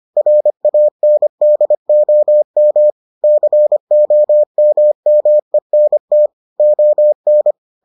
Morse code, heard as a keyed tone, about 25 wpm, 600 Hz. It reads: RANDOM COMMENT ON